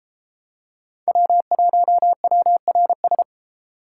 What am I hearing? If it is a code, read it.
Morse code, heard as a keyed tone, about 29 wpm, 700 Hz. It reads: W1WRS